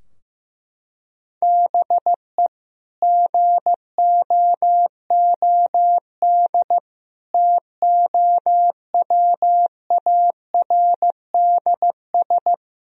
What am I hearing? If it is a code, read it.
Morse code, heard as a keyed tone, about 15 wpm, 700 Hz. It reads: BE GOOD TOWARDS
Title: Sentences from Top 1000 Words - Review Part 2 - 15wpm